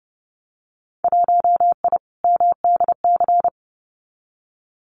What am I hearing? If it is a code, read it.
Morse code, heard as a keyed tone, about 30 wpm, 700 Hz. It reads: 1I MDC
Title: Winter Field Day Exchanges 30wpm